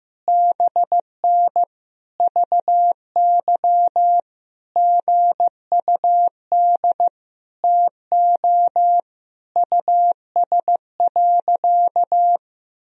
Morse code, heard as a keyed tone, about 15 wpm, 700 Hz. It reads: BN VY GUD TO US.
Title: Sentences from Top 200 Words and Abbreviations 15wpm